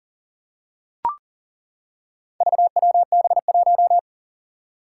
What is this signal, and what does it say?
Morse code, heard as a keyed tone, about 41 wpm, 700 Hz. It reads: VWB1
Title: Four Challenging Letter-Number Combinations 40wpm